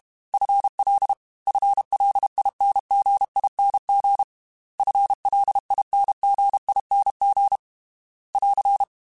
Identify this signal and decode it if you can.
Morse code, heard as a keyed tone, about 32 wpm, 800 Hz. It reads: FL FLINGING FLINGING <AR>
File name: FL / FL mp3s / FL Prefixes 3x2 Twice @30